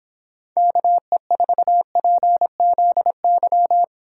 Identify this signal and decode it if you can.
Morse code, heard as a keyed tone, about 30 wpm, 700 Hz. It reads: KE4PZY